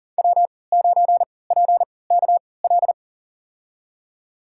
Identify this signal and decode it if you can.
Morse code, heard as a keyed tone, about 40 wpm, spaced as for 23 wpm, 700 Hz. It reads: W9PKL
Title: USA Call Signs - 5 Characters - 3x Character Spacing - 40wpm